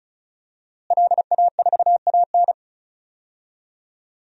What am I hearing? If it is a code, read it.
Morse code, heard as a keyed tone, about 34 wpm, 700 Hz. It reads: LA4AN
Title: DXCC-100 Call Signs - 5 Characters - Repeated 2x - 35wpm